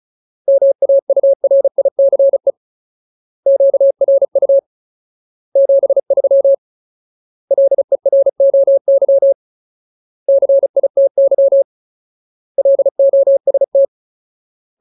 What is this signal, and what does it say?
Morse code, heard as a keyed tone, about 34 wpm, 550 Hz. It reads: MAURICE QRU 73 LEROY CITY LOST
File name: Ultimate Word List / 35WPM Ultimate